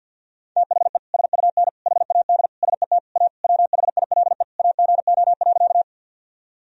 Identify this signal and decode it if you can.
Morse code, heard as a keyed tone, about 52 wpm, 700 Hz. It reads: THE SUN HAD SET A WHILE AGO.
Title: Sentences from Top 1000 Words - Review Part 1 - 50wpm